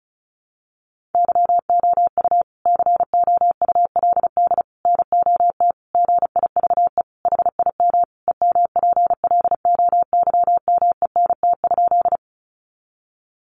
Morse code, heard as a keyed tone, about 34 wpm, 700 Hz. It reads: YOU COULD NOT GIVE HIM EMPLOYMENT?
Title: Sentences from Top 1500 Words 35wpm